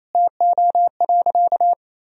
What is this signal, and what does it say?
Morse code, heard as a keyed tone, about 28 wpm, 700 Hz. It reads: TO.
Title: Sentences from Top 1000 Words - Review Part 2 - 28wpm